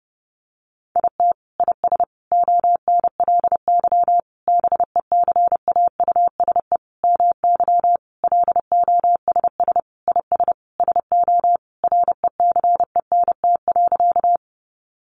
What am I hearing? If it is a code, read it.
Morse code, heard as a keyed tone, about 28 wpm, 700 Hz. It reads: IT IS ONLY BECAUSE MY LOSS IS SO RECENT.
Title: Sentences from Top 1400 Words 30wpm